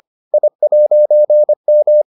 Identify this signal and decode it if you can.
Morse code, heard as a keyed tone, about 25 wpm, 600 Hz. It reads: I'M